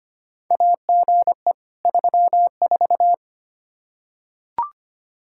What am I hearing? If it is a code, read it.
Morse code, heard as a keyed tone, about 25 wpm, 700 Hz. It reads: AGE 34
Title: K1USN SST QSO Practice 25wpm